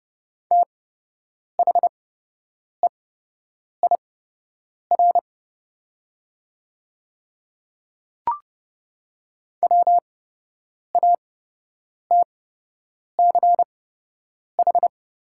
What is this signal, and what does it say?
Morse code, heard as a keyed tone, about 30 wpm, spaced as for 8 wpm, 700 Hz. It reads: THEIR WATCH
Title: 8x Character Spacing - Top 300 Words - 30wpm